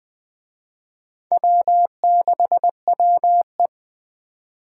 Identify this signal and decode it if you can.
Morse code, heard as a keyed tone, about 21 wpm, 700 Hz. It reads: W6WE
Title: Speed Racing - USA Call Signs 17wpm